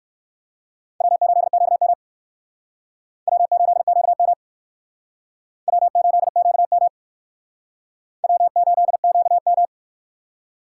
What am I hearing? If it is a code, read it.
Morse code, heard as a keyed tone, about 56 wpm, 700 Hz. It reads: W8QM W8QM W8QM W8QM
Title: Speed Racing - USA 1x2 Call Signs 45wpm